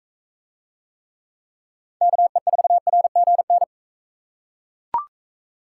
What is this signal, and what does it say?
Morse code, heard as a keyed tone, about 40 wpm, 700 Hz. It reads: KE4RGN